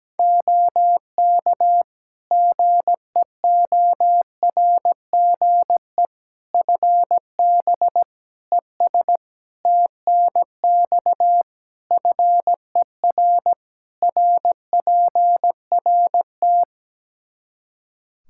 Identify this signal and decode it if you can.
Morse code, heard as a keyed tone, about 17 wpm, 700 Hz. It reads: OK GEORGE FB ES TNX FER RPRT
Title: K1USN SST QSO Practice 17wpm